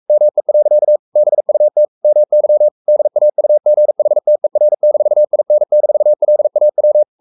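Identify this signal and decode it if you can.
Morse code, heard as a keyed tone, about 44 wpm, 600 Hz. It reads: ME. BUT MY DAUGHTER-IN-LAW
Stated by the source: Morse code at 43 wpm sending Ken Follett's novel Never in English, from CW med Gnister.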